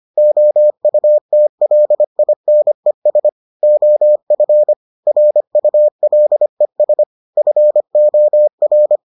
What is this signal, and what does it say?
Morse code, heard as a keyed tone, about 25 wpm, 600 Hz. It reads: OUTLINES OF RULES FOR